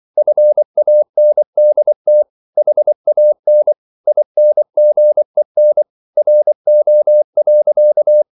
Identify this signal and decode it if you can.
Morse code, heard as a keyed tone, about 24 wpm, 600 Hz. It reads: FANDT HAN INGEN RO.
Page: Ferie 24 wpm